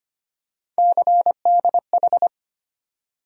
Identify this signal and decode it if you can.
Morse code, heard as a keyed tone, about 25 wpm, 700 Hz. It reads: CDH
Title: USA Call Sign Postfixes - 3 Letters 25wpm